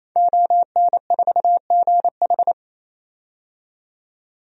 Morse code, heard as a keyed tone, about 28 wpm, 700 Hz. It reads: ON4GH
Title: DXCC-100 Call Signs - 5 Characters - 28wpm